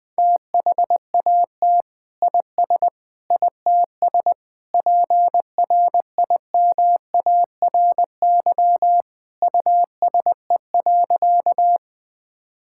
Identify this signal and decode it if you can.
Morse code, heard as a keyed tone, about 19 wpm, 700 Hz. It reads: THAT IS ITS PRIMARY USE.
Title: Sentences from Top 1000 Words - Review Part 5 - 20wpm